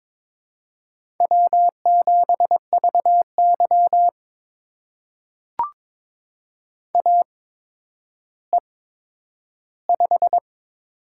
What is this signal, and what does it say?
Morse code, heard as a keyed tone, about 22 wpm, spaced as for 11 wpm, 700 Hz. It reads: W7VY AE5
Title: USA Call Signs - 4 Characters - 8x Character Spacing - 22wpm